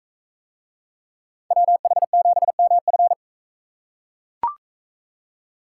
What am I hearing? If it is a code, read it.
Morse code, heard as a keyed tone, about 44 wpm, 700 Hz. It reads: WH7MF